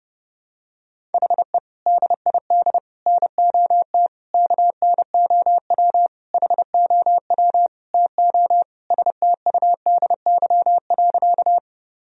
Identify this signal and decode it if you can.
Morse code, heard as a keyed tone, about 31 wpm, 700 Hz. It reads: HE DID NOT KNOW HOW TO STUDY.
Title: Sentences from Top 1000 Words - Review Part 1 - 30wpm